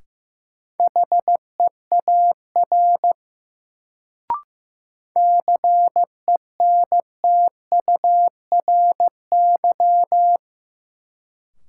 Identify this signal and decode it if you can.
Morse code, heard as a keyed tone, about 15 wpm, 700 Hz. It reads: HEAR CENTURY
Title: Sets of 5 Words - New Word Drill - 15wpm